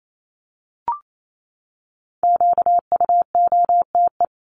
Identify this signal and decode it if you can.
Morse code, heard as a keyed tone, about 28 wpm, 700 Hz. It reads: QUOTE
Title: Top 1751-2000 Words 28wpm